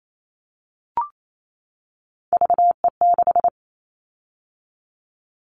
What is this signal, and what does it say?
Morse code, heard as a keyed tone, about 28 wpm, 700 Hz. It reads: VE6
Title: Callsign Prefixes for Top 100 DXCC - 28wpm